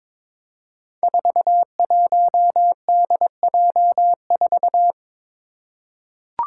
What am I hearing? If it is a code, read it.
Morse code, heard as a keyed tone, about 22 wpm, 700 Hz. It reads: 41DJ4
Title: Five Challenging Letter-Number Combinations 22wpm